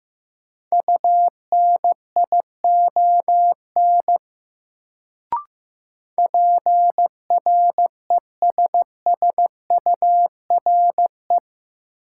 Morse code, heard as a keyed tone, about 15 wpm, 700 Hz. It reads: UNION PRESSURE
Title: Sentences from Top 1400 Words - New Word Drill Alt - 15wpm